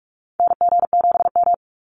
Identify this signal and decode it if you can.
Morse code, heard as a keyed tone, about 46 wpm, 700 Hz. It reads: NG7M
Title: Call Signs - 45wpm